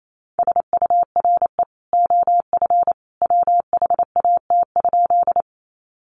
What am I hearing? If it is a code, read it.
Morse code, heard as a keyed tone, about 28 wpm, 700 Hz. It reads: SURE OF WHAT?
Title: Sentences from Top 300 Words 28wpm